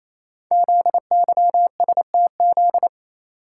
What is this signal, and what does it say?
Morse code, heard as a keyed tone, about 27 wpm, 700 Hz. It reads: ZYSTZ
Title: Code Groups - 5 Letters - 28wpm